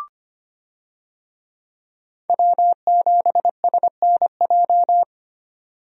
Morse code, heard as a keyed tone, about 25 wpm, 700 Hz. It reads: W7SNJ
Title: USA Call Signs - 5 Characters - 25wpm